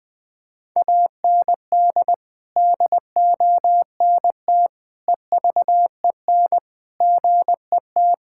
Morse code, heard as a keyed tone, about 21 wpm, 700 Hz. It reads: AND DONT EVEN GET
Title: Sentences with 8-Word Phrases - 20wpm